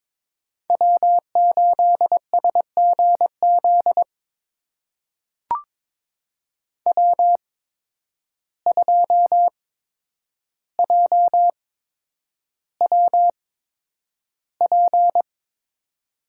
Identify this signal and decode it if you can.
Morse code, heard as a keyed tone, about 23 wpm, spaced as for 10 wpm, 700 Hz. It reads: W8SGZ W2JWP
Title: USA Call Signs - 5 Characters - 8x Character Spacing - 22wpm